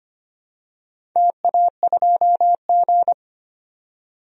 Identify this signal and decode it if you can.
Morse code, heard as a keyed tone, about 25 wpm, 700 Hz. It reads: TA2G
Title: DXCC-100 Call Signs - 4 Characters - 25wpm